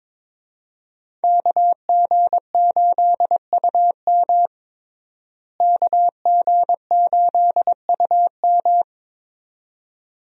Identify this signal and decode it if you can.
Morse code, heard as a keyed tone, about 22 wpm, 700 Hz. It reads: KG8UM KG8UM
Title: USA Call Signs - 5 Characters - Repeated 2x - 22wpm